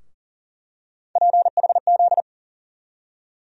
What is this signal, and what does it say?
Morse code, heard as a keyed tone, about 40 wpm, 700 Hz. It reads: PHZ
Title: USA Call Sign Postfixes - 3 Letters 40wpm